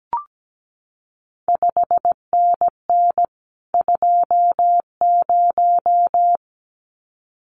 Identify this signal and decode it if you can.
Morse code, heard as a keyed tone, about 17 wpm, 700 Hz. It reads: 5NN 20
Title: CQ WW DX Contest Exchange - 17wpm